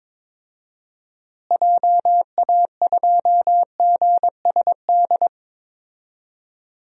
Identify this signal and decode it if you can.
Morse code, heard as a keyed tone, about 22 wpm, 700 Hz. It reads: JA2GSD
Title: DXCC-100 Call Signs - 6 Characters - Repeated 2x - 22wpm